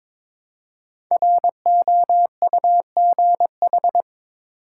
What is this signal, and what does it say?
Morse code, heard as a keyed tone, about 22 wpm, 700 Hz. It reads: ROUGH
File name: 5 Letter Words - First 3 Letters Given - 22wpm